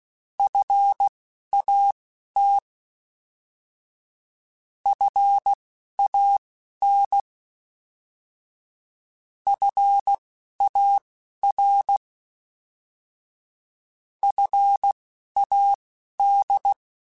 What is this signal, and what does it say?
Morse code, heard as a keyed tone, about 16 wpm, spaced as for 11 wpm, 800 Hz. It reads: FAT FAN FAR FAD
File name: Alphabetical / Alphabetical mp3s / Fs @10